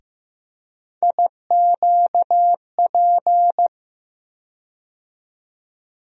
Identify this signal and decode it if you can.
Morse code, heard as a keyed tone, about 15 wpm, 700 Hz. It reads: IQP
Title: USA Call Sign Postfixes - 3 Letters 15wpm